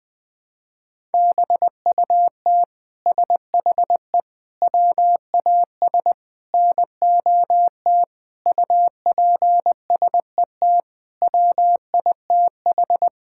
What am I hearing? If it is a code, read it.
Morse code, heard as a keyed tone, about 20 wpm, 700 Hz. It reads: BUT SHE WAS NOT UPSET WITH
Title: Sentences from Top 1000 Words - Review Part 5 - 20wpm